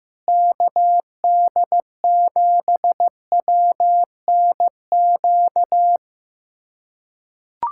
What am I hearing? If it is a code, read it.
Morse code, heard as a keyed tone, about 15 wpm, 700 Hz. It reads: KD7WNQ
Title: USA Callsign DE Callsign - 6 Characters - 15wpm